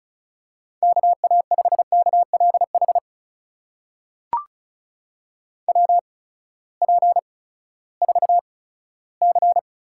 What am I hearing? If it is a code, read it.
Morse code, heard as a keyed tone, about 36 wpm, spaced as for 18 wpm, 700 Hz. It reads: KA5KLH WP4C
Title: USA Call Signs - 6 Characters - 8x Character Spacing - 35wpm